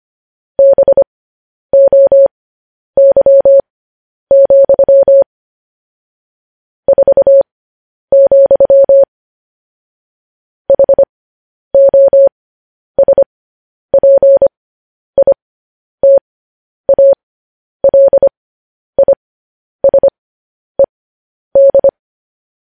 Morse code, heard as a keyed tone, about 25 wpm, spaced as for 10 wpm, 550 Hz. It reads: BOY, 4, HOSPITALISED
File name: podcast / News headlines-1x-25x10